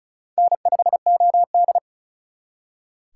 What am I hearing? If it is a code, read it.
Morse code, heard as a keyed tone, about 44 wpm, 700 Hz. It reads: N5OD